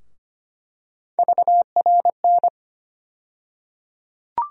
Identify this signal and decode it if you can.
Morse code, heard as a keyed tone, about 25 wpm, 700 Hz. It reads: VRN
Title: USA Call Sign Postfixes - 3 Letters 25wpm